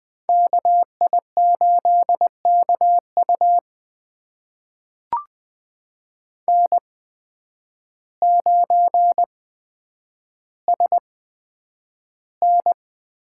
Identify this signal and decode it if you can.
Morse code, heard as a keyed tone, about 20 wpm, spaced as for 9 wpm, 700 Hz. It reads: KI8KU N9SN
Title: USA Call Signs - 8x Character Spacing - 20wpm